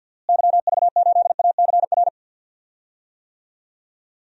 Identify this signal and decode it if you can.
Morse code, heard as a keyed tone, about 47 wpm, 700 Hz. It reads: YV8ACR
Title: DXCC-100 Call Signs - 6 Characters - Repeated 3x - 50wpm